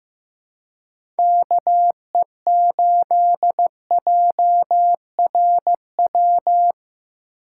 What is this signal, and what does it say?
Morse code, heard as a keyed tone, about 15 wpm, 700 Hz. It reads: KE8JRW
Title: USA Call Signs - 6 Characters - 3x Character Spacing - 15wpm